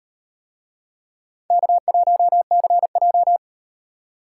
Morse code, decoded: K1CJ

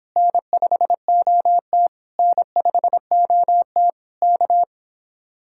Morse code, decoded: N5OT N5OT K

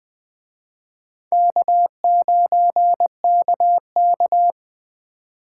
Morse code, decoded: K9KK

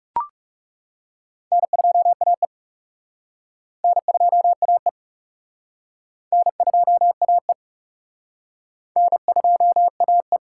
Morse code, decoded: N2AE N2AE N2AE N2AE

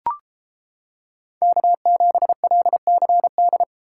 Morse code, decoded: K7LCD

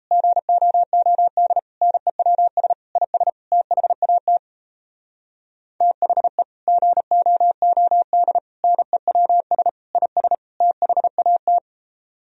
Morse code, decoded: GOOD NEWS IS THAT THE GOOD NEWS IS THAT

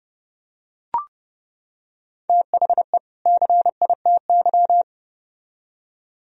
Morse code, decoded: THE CITY

